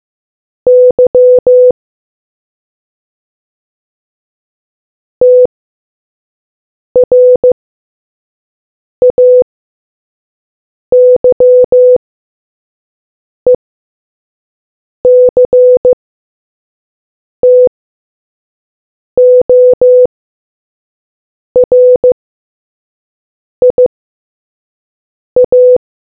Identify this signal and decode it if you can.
Morse code, heard as a keyed tone, about 15 wpm, spaced as for 5 wpm, 500 Hz. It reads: Y TRAYECTORIA